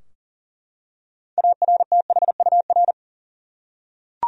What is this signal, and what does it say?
Morse code, heard as a keyed tone, about 40 wpm, 700 Hz. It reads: ARTHUR
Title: USA Common Names - 40wpm